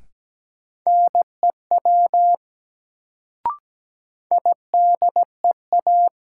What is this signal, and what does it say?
Morse code, heard as a keyed tone, about 17 wpm, 700 Hz. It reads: NEW IDEA
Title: Sentences from Top 300 Words - New Word Drill - 17wpm